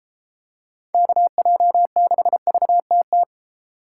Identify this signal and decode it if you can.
Morse code, heard as a keyed tone, about 31 wpm, 700 Hz. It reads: KJ6VTT